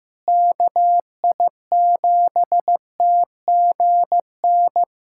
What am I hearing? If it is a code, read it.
Morse code, heard as a keyed tone, about 15 wpm, 700 Hz. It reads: KI7TGN
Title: USA Callsign DE Callsign - 6 Characters - 15wpm